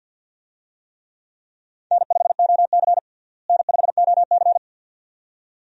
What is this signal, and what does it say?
Morse code, heard as a keyed tone, about 47 wpm, 700 Hz. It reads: N5OC N5OC
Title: USA Call Signs - 4 Characters - Repeated 2x - 50wpm